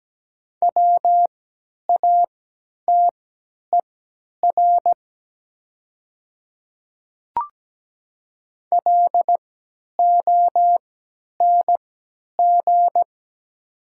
Morse code, heard as a keyed tone, about 17 wpm, spaced as for 10 wpm, 700 Hz. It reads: WATER LONG
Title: 3x Character Spacing - Top 100 Words - 17wpm